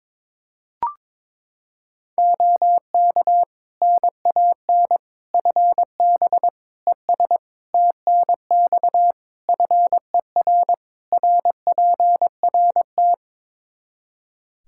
Morse code, heard as a keyed tone, about 22 wpm, 700 Hz. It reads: OK NAN FB ES TNX FER RPRT
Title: Practice QSO - Exchange 2 - Info - 22wpm